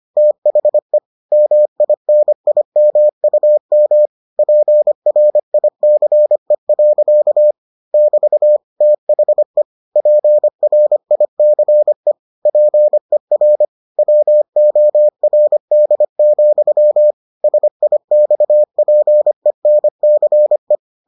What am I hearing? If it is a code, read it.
Morse code, heard as a keyed tone, about 25 wpm, 600 Hz. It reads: THE MINIMUM PRICE. <BT> THE PRICE PER WORD, SIXPENCE